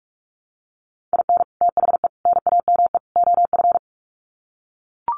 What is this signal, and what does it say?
Morse code, heard as a keyed tone, about 44 wpm, 700 Hz. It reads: IN THE NAME OF
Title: Sets of 4 Words 45wpm